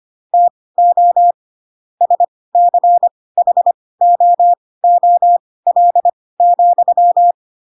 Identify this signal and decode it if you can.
Morse code, heard as a keyed tone, about 25 wpm, spaced as for 18 wpm, 700 Hz. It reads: TO SCHOOL,
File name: Pinocchio / Pinocchio ch4 12 25 ik1hge